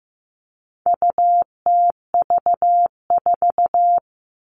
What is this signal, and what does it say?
Morse code, heard as a keyed tone, about 15 wpm, 700 Hz. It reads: UTV4
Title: All Cut Numbers - 4-Digit 15wpm